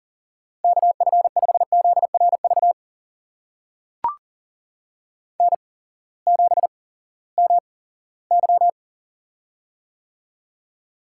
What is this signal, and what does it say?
Morse code, heard as a keyed tone, about 42 wpm, spaced as for 20 wpm, 700 Hz. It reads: KF5ZRV N7MY
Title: USA Call Signs - 8x Character Spacing - 40wpm